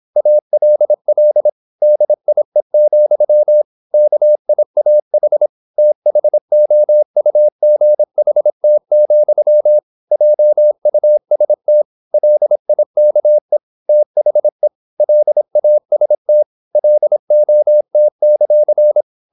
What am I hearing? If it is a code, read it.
Morse code, heard as a keyed tone, about 26 wpm, 600 Hz. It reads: ALL DIE, KIAH THOUGHT, JUST LIKE THE LAST LOT;
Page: Never 26wpm | CW med Gnister